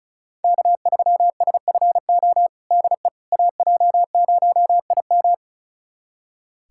Morse code, heard as a keyed tone, about 36 wpm, 700 Hz. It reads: K3SFO DE AJ0IM
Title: USA Callsign DE Callsign - 5 Characters - 35wpm